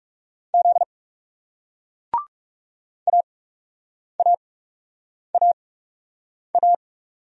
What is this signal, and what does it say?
Morse code, heard as a keyed tone, about 40 wpm, spaced as for 8 wpm, 700 Hz. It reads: Z AAAA